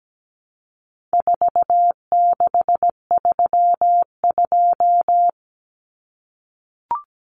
Morse code, heard as a keyed tone, about 17 wpm, 700 Hz. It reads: 4632
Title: Common Cut Numbers - 4-Digit 17wpm